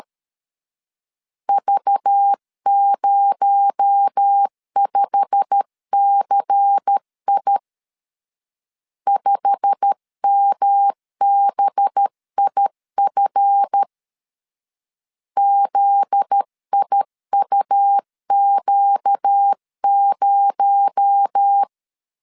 Morse code, heard as a keyed tone, about 13 wpm, 800 Hz. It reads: V05CI 5MBIF ZIUQ0
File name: LEZIONI / 8 YJ45 / 8 YJ45 12wpm 2151